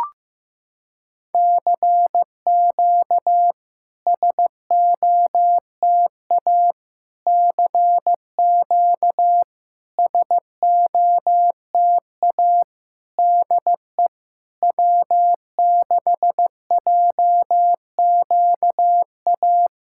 Morse code, heard as a keyed tone, about 15 wpm, 700 Hz. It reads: CQ SOTA CQ SOTA DE W6JQA
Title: SOTA QSO Practice 15wpm